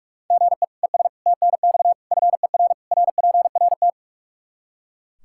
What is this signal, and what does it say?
Morse code, heard as a keyed tone, about 41 wpm, 700 Hz. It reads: GE ES TNX FER RPRT
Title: Practice QSO - Exchange 1 - Beginning - 45wpm